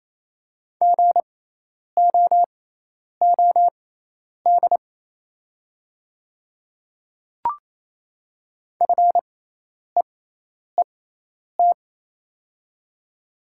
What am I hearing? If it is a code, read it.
Morse code, heard as a keyed tone, about 27 wpm, spaced as for 10 wpm, 690 Hz. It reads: GOOD FEET
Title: 6x Character Spacing - Top 500 Words - 28wpm